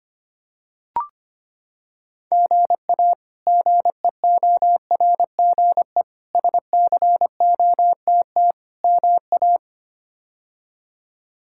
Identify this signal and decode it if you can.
Morse code, heard as a keyed tone, about 25 wpm, 700 Hz. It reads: GA GEORGE SCOTT MA
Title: K1USN SST QSO Practice - 25wpm